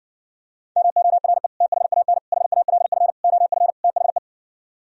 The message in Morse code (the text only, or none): MORE THAN HALF OF THE